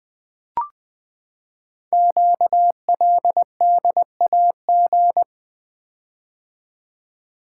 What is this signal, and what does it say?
Morse code, heard as a keyed tone, about 20 wpm, 700 Hz. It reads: QLDAG
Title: Code Groups - 5 Letters - 20wpm